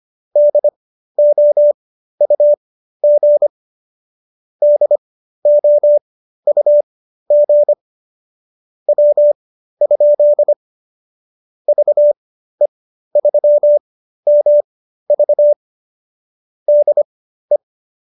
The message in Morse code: DOUG DOUG W? VE3MV DE

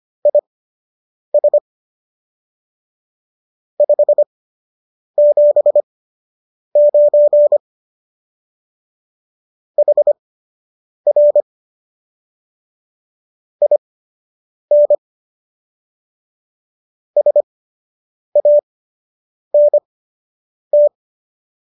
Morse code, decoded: IS 579 HR IN SANT

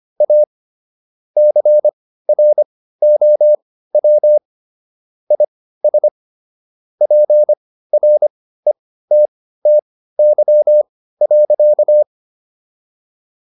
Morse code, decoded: A CROW IS PRETTY.

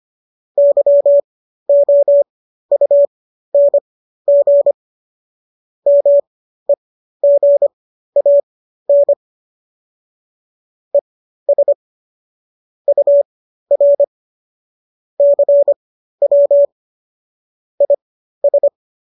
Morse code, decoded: YOUNG MEGAN ES UR CW IS

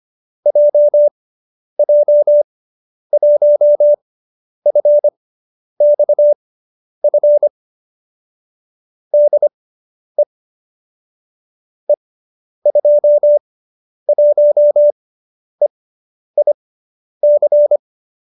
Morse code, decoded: JJ1FXF DE E21EIC

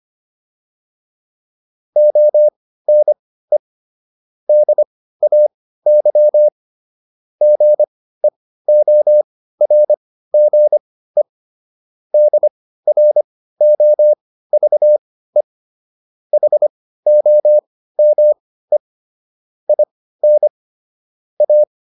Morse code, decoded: ONE DAY GEORGE DROVE HOME IN A

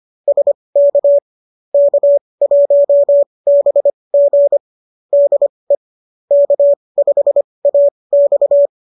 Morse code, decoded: SK K1BG DE K5AX

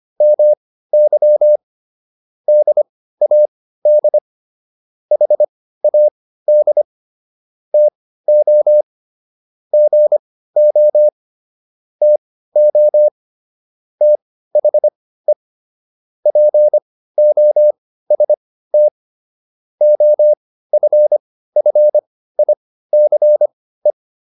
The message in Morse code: MY DAD HAD TO GO TO THE POST OFFICE

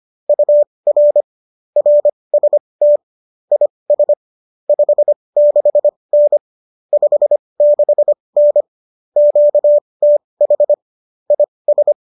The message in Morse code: UR RST IS 56N 56N QTH IS

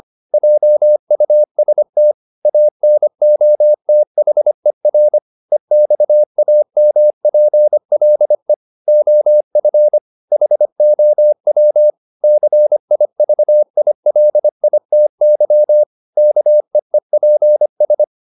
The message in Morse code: JUST ANOTHER EXAMPLE OF HOW CIVILITY KEEPS

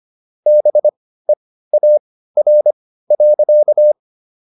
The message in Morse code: BEAR.

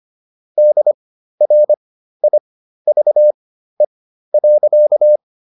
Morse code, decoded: DRIVE.

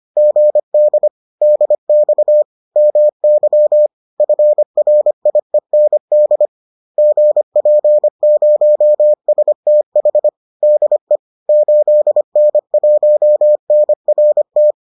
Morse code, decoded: GD DX MY FRIEND GP0STH DE 8N1NRT

